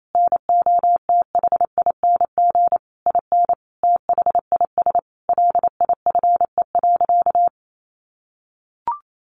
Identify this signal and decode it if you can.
Morse code, heard as a keyed tone, about 28 wpm, 700 Hz. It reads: NOTHING IN THIS LIFE.
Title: Sentences from Top 1200 Words 28wpm